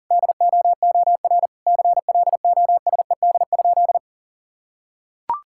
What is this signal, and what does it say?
Morse code, heard as a keyed tone, about 40 wpm, 700 Hz. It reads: DOOR CLOSED?
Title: Sentences from Top 1600 Words 40wpm